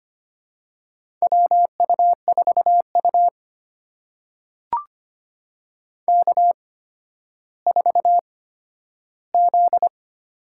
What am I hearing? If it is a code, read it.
Morse code, heard as a keyed tone, about 25 wpm, spaced as for 12 wpm, 700 Hz. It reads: WU4U K4Z
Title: USA Call Signs - 8x Character Spacing - 25wpm